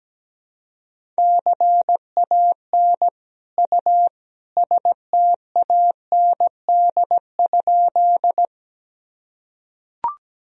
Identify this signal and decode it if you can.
Morse code, heard as a keyed tone, about 17 wpm, 700 Hz. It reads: CAN U STAND?
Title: Sentences from Top 300 Words and Abbreviations 17wpm